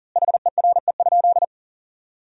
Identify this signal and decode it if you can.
Morse code, heard as a keyed tone, about 40 wpm, 700 Hz. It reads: HERE?
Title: 4x Word-Spacing - Sentences from Top 200 Words - 40wpm